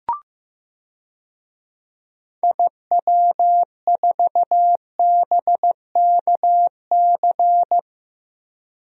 Morse code, heard as a keyed tone, about 16 wpm, 700 Hz. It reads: IW4BKC
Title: DXCC-100 Call Signs - 6 Characters - Repeated 2x - 15wpm